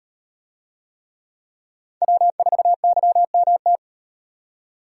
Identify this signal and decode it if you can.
Morse code, heard as a keyed tone, about 37 wpm, 700 Hz. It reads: W4YMT